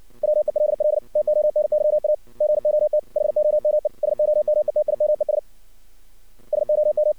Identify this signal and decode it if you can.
Morse code, heard as a keyed tone, about 59 wpm, 610 Hz. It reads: WELL TONYA GOT HOME SOMETIMES SOM
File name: archived / CRSnet